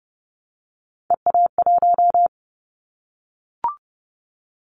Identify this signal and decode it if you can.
Morse code, heard as a keyed tone, about 30 wpm, 700 Hz. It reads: EA1
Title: Callsign Prefixes for Top 25 DXCC - 30wpm